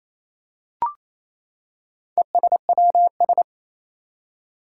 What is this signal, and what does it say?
Morse code, heard as a keyed tone, about 27 wpm, 700 Hz. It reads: ESWS
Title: Four Challenging Letter-Number Combinations 28wpm